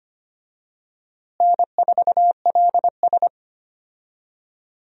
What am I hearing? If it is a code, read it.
Morse code, heard as a keyed tone, about 25 wpm, 700 Hz. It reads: N4LS
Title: USA Call Signs - 4 Characters - Repeated 3x - 25wpm